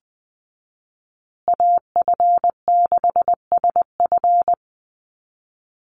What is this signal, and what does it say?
Morse code, heard as a keyed tone, about 20 wpm, 700 Hz. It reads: AF6SF
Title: USA Call Signs Repeated 3x 20wpm